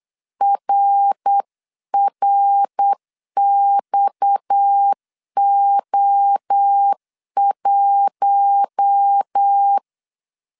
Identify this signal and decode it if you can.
Morse code, heard as a keyed tone, about 8 wpm, 800 Hz. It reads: RRXO1
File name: 8wpm